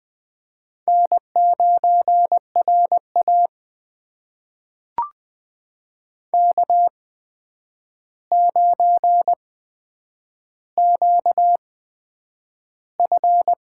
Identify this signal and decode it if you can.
Morse code, heard as a keyed tone, about 20 wpm, spaced as for 9 wpm, 700 Hz. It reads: N9RA K9QF
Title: USA Call Signs - 4 Characters - 8x Character Spacing - 20wpm